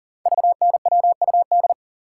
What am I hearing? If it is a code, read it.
Morse code, heard as a keyed tone, about 42 wpm, 690 Hz. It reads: VNWUD
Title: Code Groups - 5 Letters - 40wpm